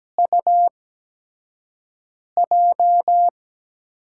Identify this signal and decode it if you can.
Morse code, heard as a keyed tone, about 17 wpm, spaced as for 5 wpm, 700 Hz. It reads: UJ
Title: USA Call Signs - 8x Character Spacing - 17wpm